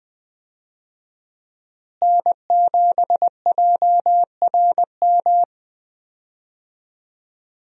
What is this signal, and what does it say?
Morse code, heard as a keyed tone, about 21 wpm, 700 Hz. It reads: N7JRM
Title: USA Call Signs - 5 Characters - Repeated 2x - 20wpm